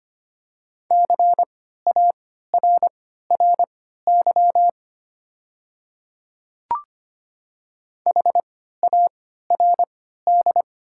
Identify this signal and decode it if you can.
Morse code, heard as a keyed tone, about 25 wpm, spaced as for 14 wpm, 700 Hz. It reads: CARRY HARD
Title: 3x Character Spacing - Top 300 Words - 25wpm